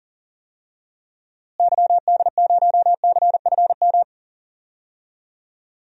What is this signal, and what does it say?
Morse code, heard as a keyed tone, about 40 wpm, 700 Hz. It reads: YD0CFM
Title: DXCC-100 Call Signs - 6 Characters - Repeated 3x - 40wpm